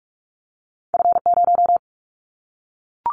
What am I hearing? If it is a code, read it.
Morse code, heard as a keyed tone, about 46 wpm, 700 Hz. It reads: F0
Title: Callsign Prefixes for Top 50 DXCC - 45wpm